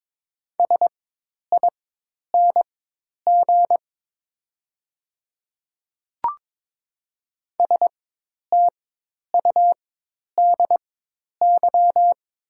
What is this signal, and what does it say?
Morse code, heard as a keyed tone, about 22 wpm, spaced as for 10 wpm, 690 Hz. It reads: SING STUDY
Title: 4x Character Spacing - Top 500 Words - 22wpm